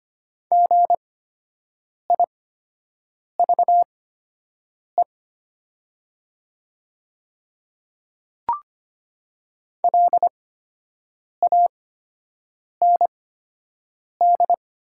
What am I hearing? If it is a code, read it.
Morse code, heard as a keyed tone, about 25 wpm, spaced as for 7 wpm, 700 Hz. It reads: GIVE LAND
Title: 8x Character Spacing - Top 200 Words - 25wpm